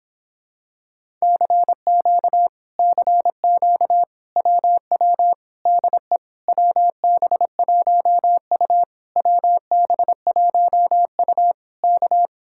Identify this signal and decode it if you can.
Morse code, heard as a keyed tone, about 26 wpm, 700 Hz. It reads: CQ CQ WW DE WB1U WB1U K